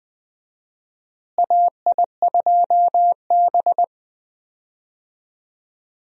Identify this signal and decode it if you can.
Morse code, heard as a keyed tone, about 20 wpm, 700 Hz. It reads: AI2B
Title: USA Call Signs - 4 Characters - Repeated 2x - 20wpm